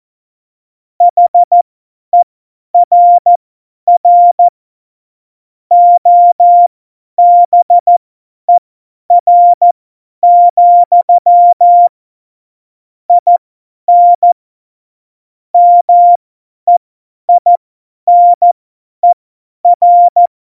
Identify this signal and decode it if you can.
Morse code, heard as a keyed tone, about 14 wpm, spaced as for 10 wpm, 700 Hz. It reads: HERR OBER, IN MEINER
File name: Morsekurs / Witze / Witz 24. September 2019, 50 F70